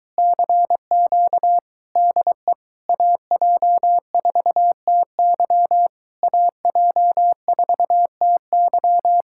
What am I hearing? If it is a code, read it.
Morse code, heard as a keyed tone, about 23 wpm, 700 Hz. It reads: CQ DE AJ4TY AJ4TY